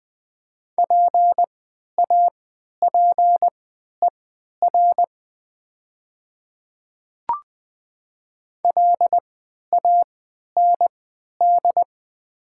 Morse code, heard as a keyed tone, about 20 wpm, spaced as for 11 wpm, 700 Hz. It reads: PAPER LAND
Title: 3x Character Spacing - Top 300 Words - 20wpm